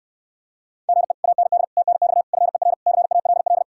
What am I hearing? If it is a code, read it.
Morse code, heard as a keyed tone, about 66 wpm, 700 Hz. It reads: GE AND TNX FER CALL